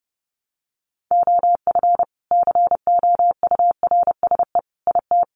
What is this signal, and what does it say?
Morse code, heard as a keyed tone, about 30 wpm, 700 Hz. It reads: OF COURSE IT
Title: Sentences from Top 500 Words 30wpm